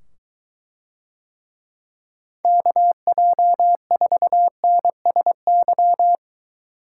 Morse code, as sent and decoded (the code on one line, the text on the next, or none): -.- .--- ....- -. ... -.--
KJ4NSY